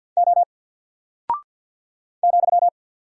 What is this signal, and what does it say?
Morse code, heard as a keyed tone, about 50 wpm, 700 Hz. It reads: O ,
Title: Single Character (Letter:Number:Punctuation:Prosign) 50wpm